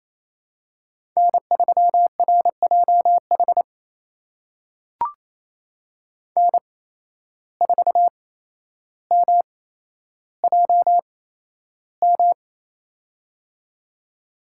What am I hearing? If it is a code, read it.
Morse code, heard as a keyed tone, about 28 wpm, spaced as for 12 wpm, 700 Hz. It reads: N3RJH N4MJM
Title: USA Call Signs - 5 Characters - 8x Character Spacing - 28wpm